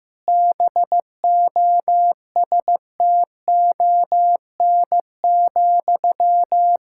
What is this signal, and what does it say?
Morse code, heard as a keyed tone, about 15 wpm, 700 Hz. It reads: BOSTON,
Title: Practice QSO - Exchange 1 - Beginning - 15wpm